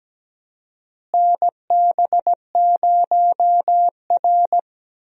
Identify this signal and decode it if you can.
Morse code, heard as a keyed tone, about 17 wpm, 700 Hz. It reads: NB0R
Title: USA Call Signs - 4 Characters - Repeated 2x - 17wpm